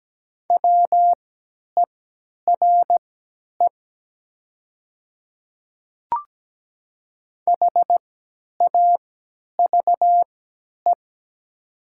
Morse code, heard as a keyed tone, about 17 wpm, spaced as for 10 wpm, 700 Hz. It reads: WERE HAVE
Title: 3x Character Spacing - Top 100 Words - 17wpm